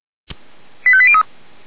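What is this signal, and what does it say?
Five-tone selective call (ZVEI), heard as quick runs of five tones, tones of about 70 ms. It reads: 85803
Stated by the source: Five-tone ZVEI selective call, address 89366, received on an SDR.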